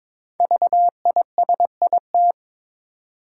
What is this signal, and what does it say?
Morse code, heard as a keyed tone, about 22 wpm, 690 Hz. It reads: VISIT
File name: 5 Letter Words - First 3 Letters Given - 22wpm